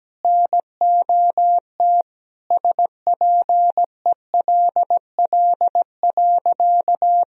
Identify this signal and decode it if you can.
Morse code, heard as a keyed tone, about 17 wpm, 700 Hz. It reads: NOT SPELL.
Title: Sentences from Top 1000 Words - Review Part 1 - 17wpm